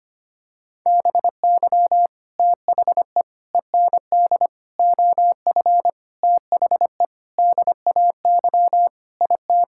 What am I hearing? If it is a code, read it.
Morse code, heard as a keyed tone, about 25 wpm, 700 Hz. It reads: BY THE END OF THE DAY IT
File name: practice / 25wpm / Sentences with 4-Word Phrases - 25wpm